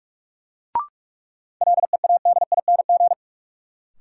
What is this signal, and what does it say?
Morse code, heard as a keyed tone, about 46 wpm, 700 Hz. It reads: LEADING
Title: Sentences from Top 1100 Words - New Word Drill Alt - 45wpm